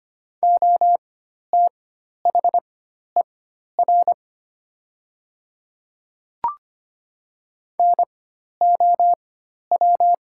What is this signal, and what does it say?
Morse code, heard as a keyed tone, about 25 wpm, spaced as for 12 wpm, 700 Hz. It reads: OTHER NOW
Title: 4x Character Spacing - Top 200 Words - 25wpm